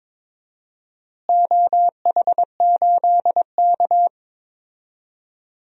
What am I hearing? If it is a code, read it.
Morse code, heard as a keyed tone, about 22 wpm, 700 Hz. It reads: OH8K
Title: DXCC-100 Call Signs - 4 Characters - Repeated 3x - 22wpm